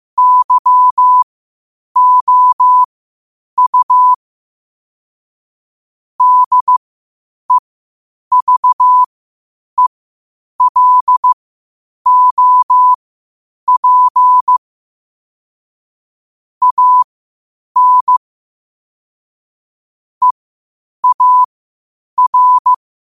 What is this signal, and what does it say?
Morse code, heard as a keyed tone, about 15 wpm, spaced as for 8 wpm, 1000 Hz. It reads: YOU DEVELOP AN EAR
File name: qotdfiles / 8wpm